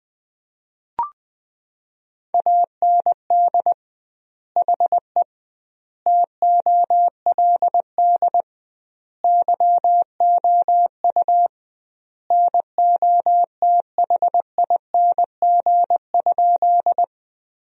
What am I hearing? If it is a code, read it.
Morse code, heard as a keyed tone, about 21 wpm, 700 Hz. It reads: AND HE TOLD YOU NOTHING?
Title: 2x Word-Spacing - Sentences from Top 500 Words - 20wpm